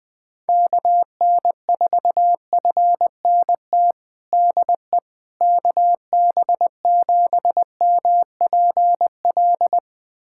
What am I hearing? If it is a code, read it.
Morse code, heard as a keyed tone, about 20 wpm, 700 Hz. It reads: KN4FNT DE KB7MPL
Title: USA Callsign DE Callsign - 6 Characters - 20wpm